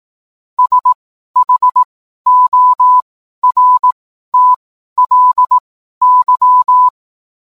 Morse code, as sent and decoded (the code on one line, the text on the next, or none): ... .... --- .-. - .-.. -.--
SHORTLY